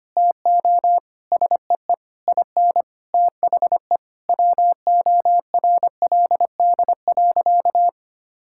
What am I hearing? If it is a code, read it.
Morse code, heard as a keyed tone, about 25 wpm, 700 Hz. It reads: TO SEE IN THE WORLD.
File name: practice / 25wpm / Sentences with 9-Word Phrases - 25wpm